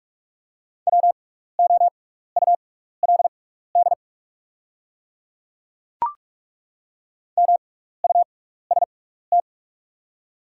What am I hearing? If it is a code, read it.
Morse code, heard as a keyed tone, about 43 wpm, spaced as for 15 wpm, 700 Hz. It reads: WOULD MUST
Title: 6x Character Spacing - Top 300 Words - 45wpm